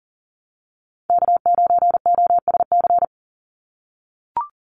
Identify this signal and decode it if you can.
Morse code, heard as a keyed tone, about 40 wpm, 700 Hz. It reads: K9OSC
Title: NA SOTA Chaser Callsigns 40wpm